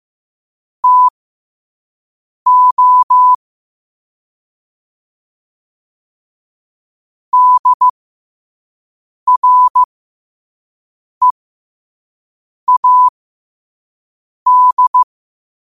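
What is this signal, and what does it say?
Morse code, heard as a keyed tone, about 15 wpm, spaced as for 5 wpm, 1000 Hz. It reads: TO DREAD
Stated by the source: Morse code quote of the day sent at 5 WPM.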